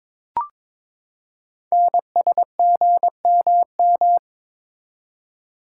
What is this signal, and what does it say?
Morse code, heard as a keyed tone, about 22 wpm, 700 Hz. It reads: NSGMM
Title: Code Groups - 5 Letters - 22wpm